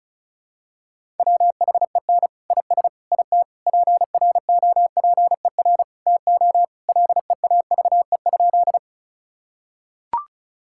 Morse code, heard as a keyed tone, about 36 wpm, 700 Hz. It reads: WHEN IS IT PROPER TO LEAVE?
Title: Sentences from Top 1000 Words - Review Part 5 - 35wpm